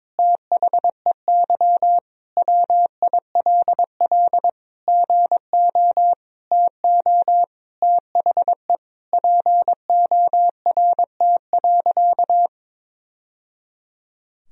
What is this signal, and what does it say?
Morse code, heard as a keyed tone, about 22 wpm, 700 Hz. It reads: THEY WILL GO TO THE PORT.
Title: Sentences from Top 1000 Words - Review Part 1 - 22wpm